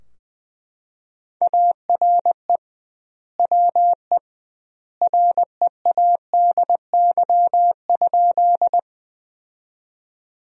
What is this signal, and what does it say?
Morse code, heard as a keyed tone, about 21 wpm, 700 Hz. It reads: ARE WE READY?
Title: 2x Word-Spacing - Sentences from Top 400 Words - 20wpm